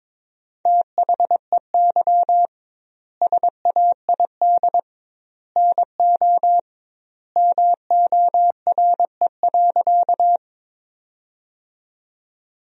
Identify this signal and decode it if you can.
Morse code, heard as a keyed tone, about 22 wpm, 700 Hz. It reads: THEY SAID NO MORE.
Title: 2x Word-Spacing - Sentences from Top 100 Words - 22wpm